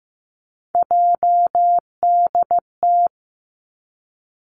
Morse code, heard as a keyed tone, about 15 wpm, 700 Hz. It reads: JDT
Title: Three Challenging Letter-Number Combinations 15wpm